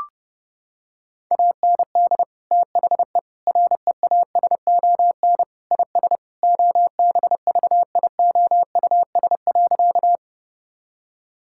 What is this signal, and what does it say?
Morse code, heard as a keyed tone, about 30 wpm, 700 Hz. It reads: AND THE REASON IS OBVIOUS.
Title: Sentences from Top 1600 Words 30wpm